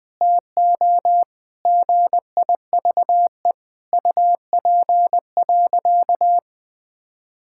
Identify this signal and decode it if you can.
Morse code, heard as a keyed tone, about 20 wpm, 700 Hz. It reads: TO GIVE UP.
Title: Sentences with 5-Word Phrases - Spoken Beginning - 20wpm